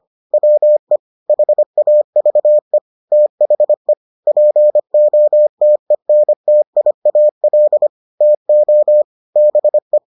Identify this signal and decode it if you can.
Morse code, heard as a keyed tone, about 25 wpm, 600 Hz. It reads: WE HAVE THE POTENTIAL TO BE